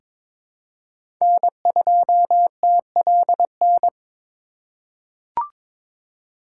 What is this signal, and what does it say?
Morse code, heard as a keyed tone, about 23 wpm, 700 Hz. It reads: N2TLN
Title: USA Call Signs - 5 Characters - 3x Character Spacing - 22wpm